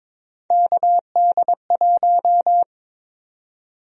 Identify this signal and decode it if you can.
Morse code, heard as a keyed tone, about 22 wpm, 700 Hz. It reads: KD1
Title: USA Call Sign Prefixes - 22wpm